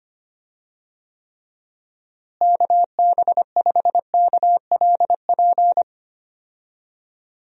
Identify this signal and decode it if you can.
Morse code, heard as a keyed tone, about 25 wpm, 700 Hz. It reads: KB5KLP
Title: USA Call Signs - 6 Characters - Repeated 2x - 25wpm